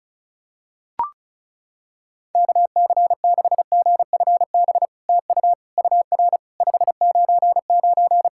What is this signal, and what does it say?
Morse code, heard as a keyed tone, about 36 wpm, 700 Hz. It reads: KC6GFB TU UR 599
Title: SOTA QSO Practice 35wpm